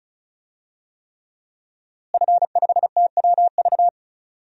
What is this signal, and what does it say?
Morse code, heard as a keyed tone, about 36 wpm, 700 Hz. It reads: F5TWV
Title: DXCC-100 Call Signs - 5 Characters - Repeated 3x - 35wpm